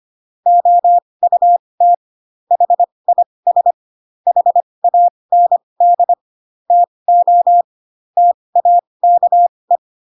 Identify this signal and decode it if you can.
Morse code, heard as a keyed tone, about 25 wpm, spaced as for 20 wpm, 700 Hz. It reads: OUT HIS HAND TO TAKE